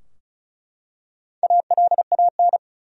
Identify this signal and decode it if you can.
Morse code, heard as a keyed tone, about 35 wpm, 700 Hz. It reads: ALAN
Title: USA Common Names - Max 4 Letters - 35wpm